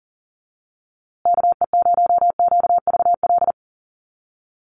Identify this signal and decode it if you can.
Morse code, heard as a keyed tone, about 40 wpm, 700 Hz. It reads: KE0QVL
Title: USA Call Signs Repeated 2x 40wpm